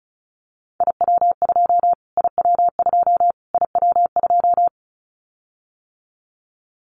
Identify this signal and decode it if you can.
Morse code, heard as a keyed tone, about 34 wpm, 700 Hz. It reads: IW2 IW2 IW2
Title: Callsign Prefixes for Top 25 DXCC - Repeated 3x - 35wpm